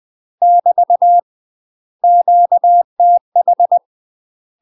Morse code, heard as a keyed tone, about 20 wpm, 700 Hz. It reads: <BT> QTH
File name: UK MASTER QSOs TRAINING / 20WPM / UK MASTER QSOs 0072